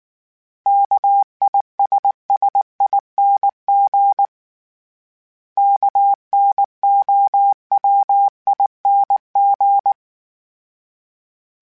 Morse code, heard as a keyed tone, about 19 wpm, 800 Hz. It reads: KISSING KNOWING
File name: Special INGs / Special INGs-mp3s / Sngl Ing @18